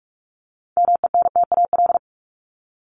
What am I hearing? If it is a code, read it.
Morse code, heard as a keyed tone, about 44 wpm, 700 Hz. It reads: MENTAL
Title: Top 501-750 Words 45wpm